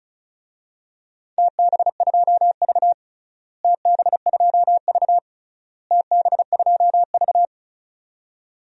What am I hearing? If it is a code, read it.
Morse code, heard as a keyed tone, about 34 wpm, 700 Hz. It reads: TB2V TB2V TB2V
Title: DXCC-100 Call Signs - 4 Characters - Repeated 3x - 35wpm